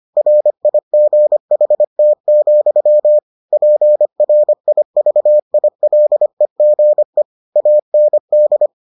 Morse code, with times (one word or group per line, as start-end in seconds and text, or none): 0.2-3.2 RIGHT,
3.5-7.2 PRIVILEGE
7.6-8.7 AND